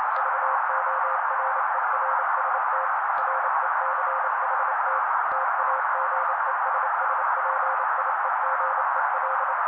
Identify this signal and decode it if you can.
Morse code, heard as a keyed tone, about 28 wpm, 530 Hz. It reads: VORERST RECHT TAGESSPIEGEL
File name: daily / 27wpm noise